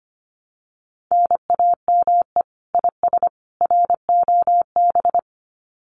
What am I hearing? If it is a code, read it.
Morse code, heard as a keyed tone, about 25 wpm, 700 Hz. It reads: NAME IS ROB
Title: Common QSO Phrases 25wpm